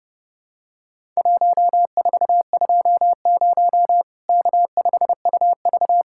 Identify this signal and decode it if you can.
Morse code, heard as a keyed tone, about 32 wpm, 700 Hz. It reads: 1420 K5UV
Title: Worked All Europe DX Contest - QTC Exchange - 30wpm